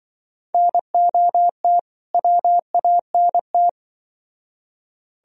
Morse code, heard as a keyed tone, about 25 wpm, 700 Hz. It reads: NOT WANT
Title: Speed-Racing - Sets of 2 Words 20wpm